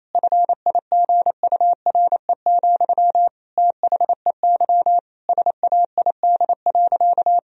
Morse code, heard as a keyed tone, about 28 wpm, 700 Hz. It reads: FIGURE, THEY SAID.
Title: Sentences from Top 1000 Words - Review Part 2 - 28wpm